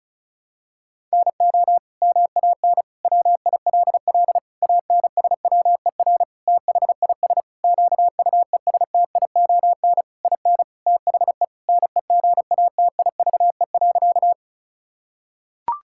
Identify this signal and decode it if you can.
Morse code, heard as a keyed tone, about 34 wpm, 700 Hz. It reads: NO MAN WILL ANSWER THIS QUESTION IN THE NEGATIVE.
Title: Sentences from Top 1000 Words - Review Part 6 - 35wpm